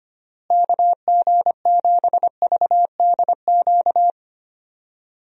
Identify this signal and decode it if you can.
Morse code, heard as a keyed tone, about 25 wpm, 700 Hz. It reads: KG7VDQ
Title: USA Call Signs - 6 Characters - Repeated 2x - 25wpm